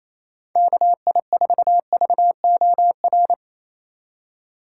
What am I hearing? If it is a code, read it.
Morse code, heard as a keyed tone, about 28 wpm, 700 Hz. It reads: KI4VOR
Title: USA Call Signs - 6 Characters - 2x Character Spacing - 28wpm